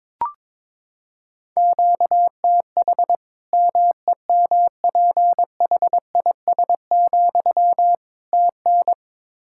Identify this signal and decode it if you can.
Morse code, heard as a keyed tone, about 21 wpm, 700 Hz. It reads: QTH MEMPHIS, TN
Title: Practice QSO - Exchange 1 - Beginning - 22wpm